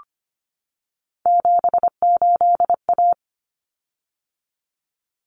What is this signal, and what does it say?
Morse code, heard as a keyed tone, about 25 wpm, 700 Hz. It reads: 78A
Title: Common Cut Numbers - 3-Digit 25wpm